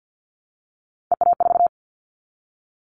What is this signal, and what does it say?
Morse code, heard as a keyed tone, about 48 wpm, 700 Hz. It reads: EA4
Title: Callsign Prefixes for Top 25 DXCC - Repeated 3x - 50wpm